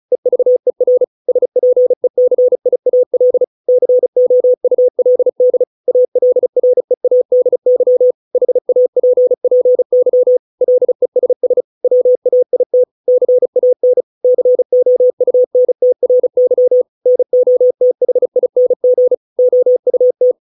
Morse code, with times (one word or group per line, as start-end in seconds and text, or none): 0.1-1.0 EVER
1.3-3.4 SPECIAL
3.7-5.6 COULD
5.9-8.1 ALREADY
8.3-10.4 HAPPY
10.6-11.6 LESS
11.8-12.8 WAIT
13.1-14.0 CAN
14.2-16.8 COUNTRY
17.1-19.1 NOTHING
19.4-20.3 OUT